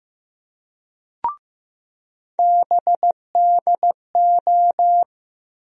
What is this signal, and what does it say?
Morse code, heard as a keyed tone, about 15 wpm, 700 Hz. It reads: BDO
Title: USA Call Sign Postfixes - 3 Letters 15wpm